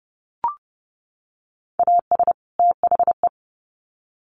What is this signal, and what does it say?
Morse code, heard as a keyed tone, about 30 wpm, 700 Hz. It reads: AS THE
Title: Sets of 2 Words 30wpm